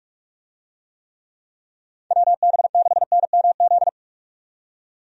Morse code, heard as a keyed tone, about 47 wpm, 700 Hz. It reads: WB6NMZ
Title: USA Call Signs - 6 Characters - 45wpm